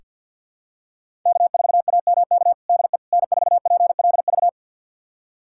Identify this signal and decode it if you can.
Morse code, heard as a keyed tone, about 46 wpm, 700 Hz. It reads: K4AMK DE N4PLV
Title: USA Callsign DE Callsign - 5 Characters - 50wpm